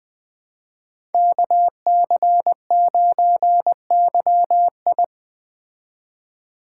Morse code, heard as a keyed tone, about 20 wpm, 700 Hz. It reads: KC9YI
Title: Speed Racing - USA Call Signs 17wpm